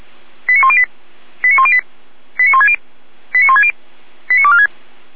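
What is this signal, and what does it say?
Five-tone selective call (ZVEI), heard as quick runs of five tones, tones of about 70 ms. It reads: 89108 89108 89160 89160 89246
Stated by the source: ZVEI five-tone call 89169 received on SDR.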